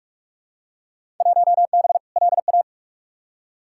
Morse code, heard as a keyed tone, about 47 wpm, 700 Hz. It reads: 1B LA
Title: Field Day Exchanges - 45wpm